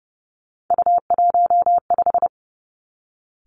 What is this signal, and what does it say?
Morse code, heard as a keyed tone, about 30 wpm, 700 Hz. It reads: U15
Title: Three Challenging Letter-Number Combinations 30wpm